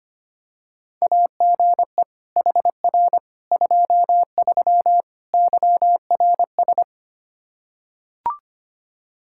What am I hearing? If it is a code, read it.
Morse code, heard as a keyed tone, about 25 wpm, 700 Hz. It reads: AGE HR 23 YRS
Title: K1USN SST QSO Practice 25wpm